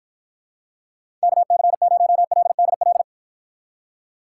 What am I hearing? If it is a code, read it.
Morse code, heard as a keyed tone, about 53 wpm, 700 Hz. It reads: KC0LDL